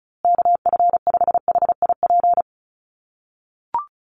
Morse code, heard as a keyed tone, about 34 wpm, 700 Hz. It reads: KF5HIP
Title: USA Call Signs Repeated 3x 35wpm